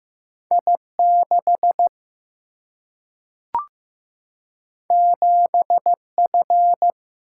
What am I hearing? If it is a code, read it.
Morse code, heard as a keyed tone, about 15 wpm, 700 Hz. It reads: I 6 7 F
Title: Single-Letter Single-Number 15wpm